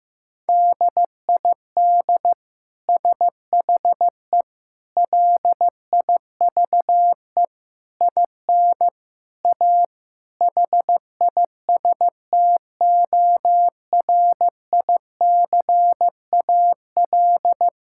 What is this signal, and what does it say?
Morse code, heard as a keyed tone, about 15 wpm, 700 Hz. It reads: DID SHE LIVE IN A HISTORICAL
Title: Sentences from Top 1000 Words - Review Part 5 - 15wpm